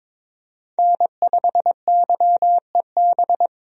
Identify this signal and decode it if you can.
Morse code, heard as a keyed tone, about 22 wpm, 700 Hz. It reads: N5YEB
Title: USA Call Signs - 5 Characters - 22wpm